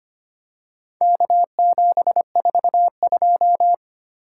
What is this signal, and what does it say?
Morse code, heard as a keyed tone, about 25 wpm, 700 Hz. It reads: K742
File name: practice / 25wpm / Four Letter-Number Combinations - 25wpm